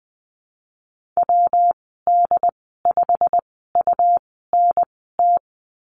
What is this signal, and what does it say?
Morse code, heard as a keyed tone, about 20 wpm, spaced as for 15 wpm, 700 Hz. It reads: WD5UNT
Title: 140 Lesson 23g - Letter f - Call Signs 20wpm